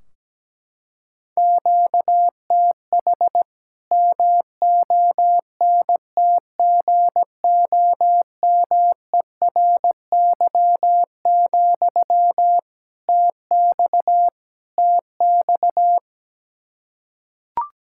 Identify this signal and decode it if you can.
Morse code, heard as a keyed tone, about 17 wpm, 700 Hz. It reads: QTH MONTGOMERY, TX TX
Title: K1USN SST QSO Practice 17wpm